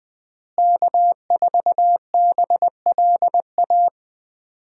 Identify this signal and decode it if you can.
Morse code, heard as a keyed tone, about 20 wpm, 700 Hz. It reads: K4BLA
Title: USA Call Signs - 5 Characters - Repeated 3x - 20wpm